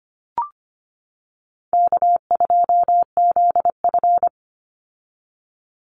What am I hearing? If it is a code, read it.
Morse code, heard as a keyed tone, about 25 wpm, 700 Hz. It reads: K2ZF
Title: Call Signs - 25wpm